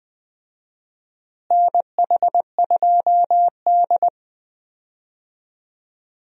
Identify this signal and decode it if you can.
Morse code, heard as a keyed tone, about 20 wpm, 700 Hz. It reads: NH2D
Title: USA Call Signs - 4 Characters - Repeated 3x - 20wpm